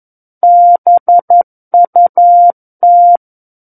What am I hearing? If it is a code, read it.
Morse code, heard as a keyed tone, about 11 wpm, 700 Hz. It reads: BUT
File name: 11 WPM